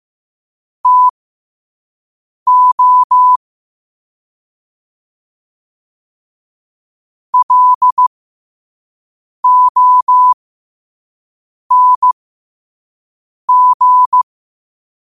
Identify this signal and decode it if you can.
Morse code, heard as a keyed tone, about 15 wpm, spaced as for 5 wpm, 1000 Hz. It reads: TO LONG
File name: qotdfiles / 5wpm